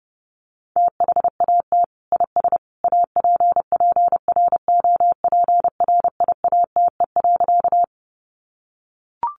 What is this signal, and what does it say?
Morse code, heard as a keyed tone, about 30 wpm, 700 Hz. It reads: THAT IS APPROPRIATE.
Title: Sentences from Top 900 Words 30wpm